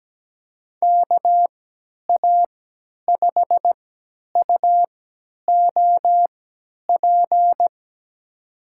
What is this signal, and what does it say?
Morse code, heard as a keyed tone, about 17 wpm, spaced as for 10 wpm, 700 Hz. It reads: KA5UOP
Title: USA Call Signs - 6 Characters - 3x Character Spacing - 17wpm